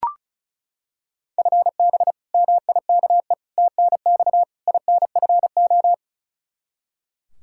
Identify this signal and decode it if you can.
Morse code, heard as a keyed tone, about 36 wpm, 700 Hz. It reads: FB MIKE TNX INFO
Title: Practice QSO - Exchange 3 - More Info - 35wpm